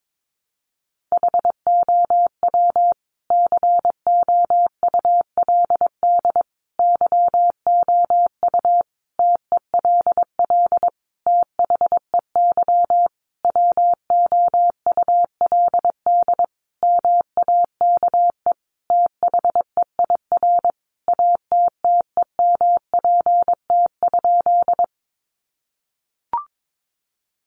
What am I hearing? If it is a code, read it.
Morse code, heard as a keyed tone, about 22 wpm, 700 Hz. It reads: HOW COULD YOU TELL THEY WOULD MAKE THEIR ATTEMPT?
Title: Sentences from Top 1000 Words 22wpm